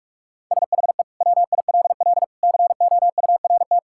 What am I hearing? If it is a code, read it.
Morse code, heard as a keyed tone, about 46 wpm, 700 Hz. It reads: SHE WILL COURT
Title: Sentences from Top 1600 Words 45wpm